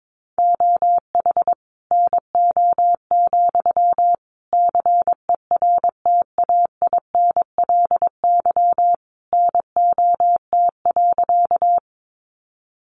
Morse code, heard as a keyed tone, about 22 wpm, 700 Hz. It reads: OH NO, CERTAINLY NOT.
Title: Sentences from Top 900 Words 22wpm